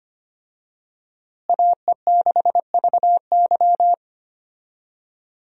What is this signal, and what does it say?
Morse code, heard as a keyed tone, about 25 wpm, 700 Hz. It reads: AE6VY
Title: USA Call Signs - 5 Characters - Repeated 3x - 25wpm